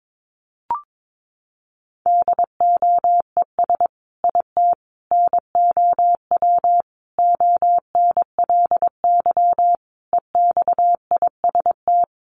Morse code, heard as a keyed tone, about 22 wpm, 700 Hz. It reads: DOES IT NOW ONLY EXIST
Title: Sentences from Top 1300 Words 22wpm